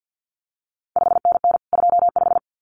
Morse code, heard as a keyed tone, about 50 wpm, 700 Hz. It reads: 5NN 25
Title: CQ WW DX Contest Exchange - 50wpm